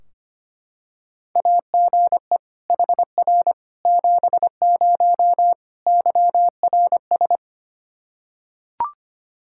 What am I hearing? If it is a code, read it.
Morse code, heard as a keyed tone, about 25 wpm, 700 Hz. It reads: AGE HR 70 YRS
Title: Practice QSO - Exchange 3 - More Info - 25wpm